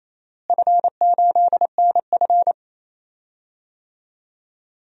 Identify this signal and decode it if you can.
Morse code, heard as a keyed tone, about 28 wpm, 700 Hz. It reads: F8NF
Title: DXCC-100 Call Signs - 4 Characters - Repeated 3x - 28wpm